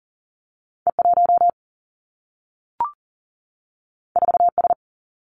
Morse code, heard as a keyed tone, about 39 wpm, 700 Hz. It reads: E 1 4 S
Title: Two Challenging Letter-Number Combinations 40wpm